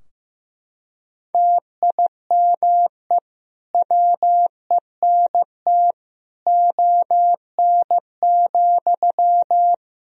Morse code, heard as a keyed tone, about 15 wpm, 700 Hz. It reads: TIME WENT ON,
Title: Sentences from Top 1000 Words - Review Part 3 - 15wpm